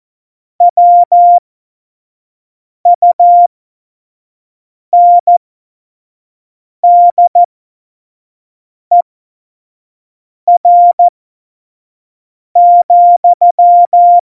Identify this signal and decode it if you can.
Morse code, heard as a keyed tone, about 14 wpm, spaced as for 5 wpm, 700 Hz. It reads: WUNDER,
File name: Morsekurs / Witze / Witz 17.Dezember 2019, 25 F70